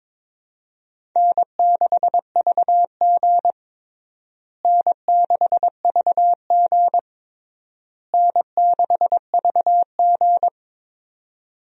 Morse code, heard as a keyed tone, about 22 wpm, 700 Hz. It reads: N6VG N6VG N6VG
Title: USA Call Signs - 4 Characters - Repeated 3x - 22wpm